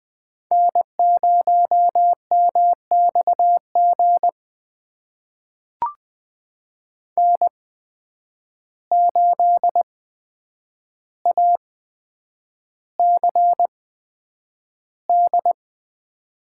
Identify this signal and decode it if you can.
Morse code, heard as a keyed tone, about 20 wpm, spaced as for 9 wpm, 700 Hz. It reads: N0MXG N8ACD
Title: USA Call Signs - 5 Characters - 8x Character Spacing - 20wpm